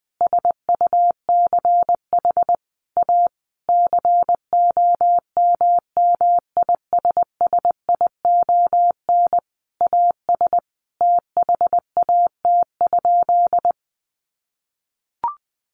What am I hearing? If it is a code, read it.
Morse code, heard as a keyed tone, about 20 wpm, 700 Hz. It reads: SUCH A COMMISSION AS THAT?
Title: Sentences from Top 1400 Words 20wpm